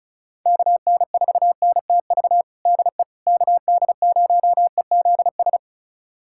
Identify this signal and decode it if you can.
Morse code, heard as a keyed tone, about 34 wpm, 700 Hz. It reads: KN4NTV DE KD0EZS
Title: USA Callsign DE Callsign 35wpm